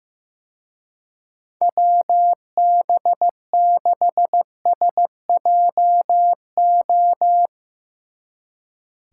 Morse code, heard as a keyed tone, about 16 wpm, 700 Hz. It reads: WB6SJO
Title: Speed Racing - USA Call Signs 15wpm